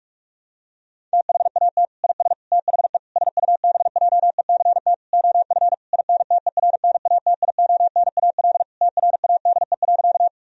Morse code, heard as a keyed tone, about 46 wpm, 700 Hz. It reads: THAT IS THE SUBJECT OF INTERNATIONAL TRADE.
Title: Sentences from Top 1000 Words - Review Part 4 - 45wpm